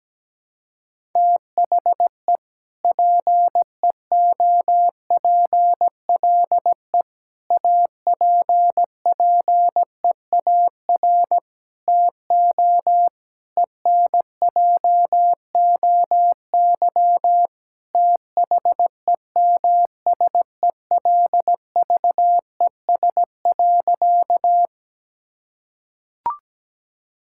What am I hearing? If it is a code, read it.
Morse code, heard as a keyed tone, about 17 wpm, 700 Hz. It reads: THE PEOPLE APPEAR TO ENJOY THEMSELVES.
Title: Sentences from Top 1000 Words - Review Part 5 - 17wpm